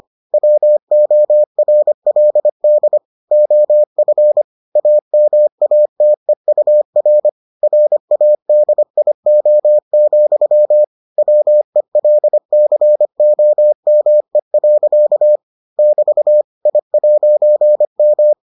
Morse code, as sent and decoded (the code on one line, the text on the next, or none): .-- --- .-. .-.. -.. / --- ..-. / .- -- .- - . ..- .-. / .-. .- -.. .. --- --..-- / .-- . .-.. -.-. --- -- . .-.-.- / -...- / .. .----. --
WORLD OF AMATEUR RADIO, WELCOME. <BT> I'M